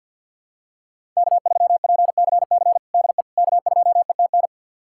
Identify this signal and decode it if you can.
Morse code, heard as a keyed tone, about 48 wpm, 700 Hz. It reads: K3PCC DE K2ETN
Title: USA Callsign DE Callsign - 5 Characters - 50wpm